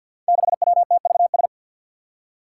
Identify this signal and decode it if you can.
Morse code, heard as a keyed tone, about 52 wpm, 700 Hz. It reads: 6WTVS
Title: Five Challenging Letter-Number Combinations 50wpm